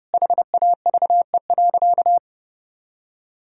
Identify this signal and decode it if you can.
Morse code, heard as a keyed tone, about 30 wpm, 700 Hz. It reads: HAVE.
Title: Sentences from Top 1000 Words - Review Part 3 - 30wpm